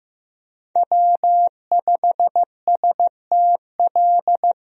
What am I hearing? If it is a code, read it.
Morse code, heard as a keyed tone, about 15 wpm, 700 Hz. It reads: W5STL
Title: USA Call Signs - 5 Characters - 15wpm